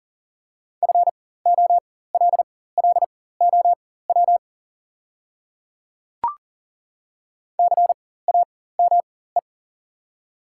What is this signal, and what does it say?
Morse code, heard as a keyed tone, about 40 wpm, spaced as for 19 wpm, 700 Hz. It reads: FOLLOW CAME
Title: 4x Character Spacing - Top 200 Words - 40wpm